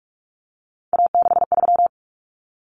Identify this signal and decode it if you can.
Morse code, heard as a keyed tone, about 44 wpm, 700 Hz. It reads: A63
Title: Common Cut Numbers - 3-Digit 45wpm